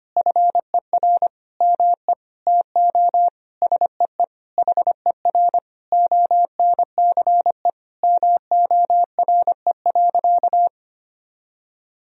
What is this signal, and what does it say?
Morse code, heard as a keyed tone, about 25 wpm, 700 Hz. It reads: FER ME TO SEE HER ONCE MORE.
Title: Sentences from Top 500 Words and Abbreviations 25wpm